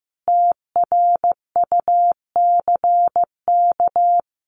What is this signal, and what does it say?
Morse code, heard as a keyed tone, about 15 wpm, 700 Hz. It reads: TRUCK
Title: Common QSO Phrases Encore 15wpm